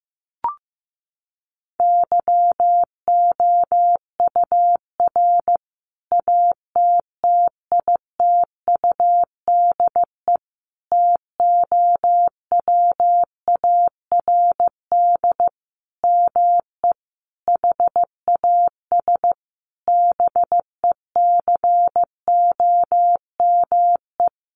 Morse code, heard as a keyed tone, about 15 wpm, 700 Hz. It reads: YOUR ATTITUDE TOWARD ME HAS BECOME
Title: Sentences from Top 1300 Words 15wpm